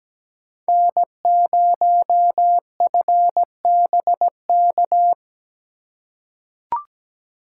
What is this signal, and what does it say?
Morse code, heard as a keyed tone, about 17 wpm, 700 Hz. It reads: N0FBK
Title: USA Call Signs - 5 Characters - 4x Character Spacing - 17wpm